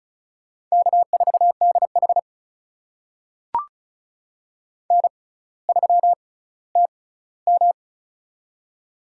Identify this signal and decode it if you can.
Morse code, heard as a keyed tone, about 36 wpm, spaced as for 18 wpm, 700 Hz. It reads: K4DH N3TM
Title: USA Call Signs - 4 Characters - 6x Character Spacing - 35wpm